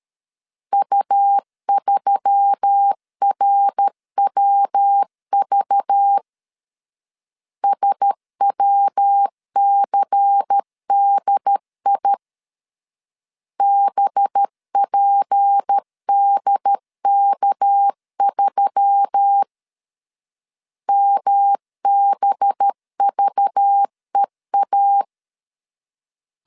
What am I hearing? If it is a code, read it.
Morse code, heard as a keyed tone, about 13 wpm, 800 Hz. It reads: U3RWV SWCDI BPDK3 MBVEA
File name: LEZIONI / 6 PKX0 / 6 PKX 12wpm 2024